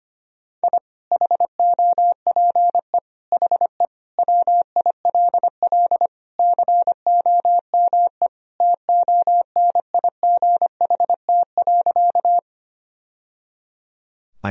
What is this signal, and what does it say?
Morse code, heard as a keyed tone, about 25 wpm, 700 Hz. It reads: I HOPE HE WILL COME TONIGHT.
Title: Sentences from Top 1000 Words - Review Part 3 - 25wpm